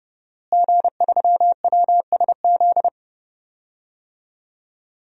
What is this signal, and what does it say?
Morse code, heard as a keyed tone, about 30 wpm, 700 Hz. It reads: G3WSZ
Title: DXCC-100 Call Signs - 5 Characters - Repeated 3x - 30wpm